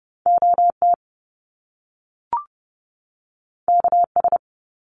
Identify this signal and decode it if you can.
Morse code, heard as a keyed tone, about 30 wpm, 700 Hz. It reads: OT KS
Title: Top Two Letter Combinations in Words 30wpm